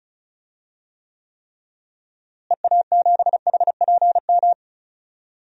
Morse code, heard as a keyed tone, about 34 wpm, 700 Hz. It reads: EA7HPM
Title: DXCC-100 Call Signs - 6 Characters - Repeated 3x - 35wpm